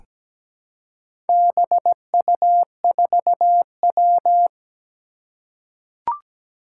Morse code, heard as a keyed tone, about 17 wpm, 700 Hz. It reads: BU4W
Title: Four Challenging Letter-Number Combinations 17wpm